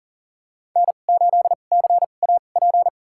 Morse code, heard as a keyed tone, about 40 wpm, spaced as for 29 wpm, 700 Hz. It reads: N8CAP
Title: USA Call Signs - 2x Character Spacing - 40wpm